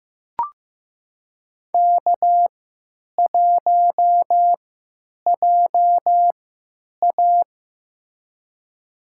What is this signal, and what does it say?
Morse code, heard as a keyed tone, about 15 wpm, spaced as for 9 wpm, 700 Hz. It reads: K1JA
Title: USA Call Signs - 3x Character Spacing - 15wpm